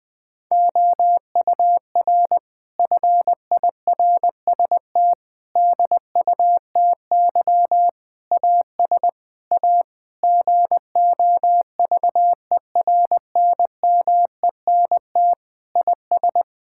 Morse code, heard as a keyed tone, about 20 wpm, 700 Hz. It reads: OUR FIRST DUTY AS A GOVERNMENT IS
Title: Sentences from Top 1600 Words 20wpm